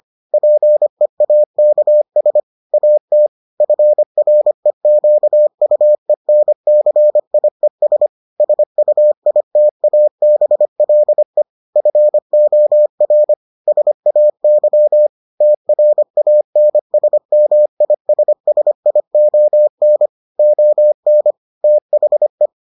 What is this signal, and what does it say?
Morse code, heard as a keyed tone, about 25 wpm, 600 Hz. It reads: PEAKS AT FREQUENCIES SUITABLE FOR SAY TRANSMISSION ON THE